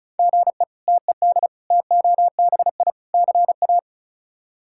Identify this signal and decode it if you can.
Morse code, heard as a keyed tone, about 36 wpm, 700 Hz. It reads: GE TED TOBI CA
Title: K1USN SST QSO Practice - 35wpm